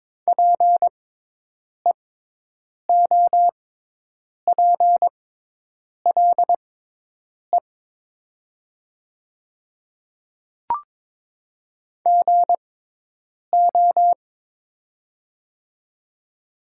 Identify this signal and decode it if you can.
Morse code, heard as a keyed tone, about 22 wpm, spaced as for 8 wpm, 700 Hz. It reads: PEOPLE GO
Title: 6x Character Spacing - Top 200 Words - 22wpm